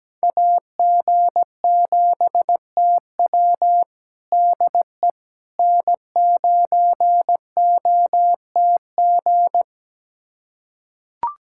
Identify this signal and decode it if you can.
Morse code, heard as a keyed tone, about 17 wpm, 700 Hz. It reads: AG7TW DE N9OTG
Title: USA Callsign DE Callsign - 5 Characters - 17wpm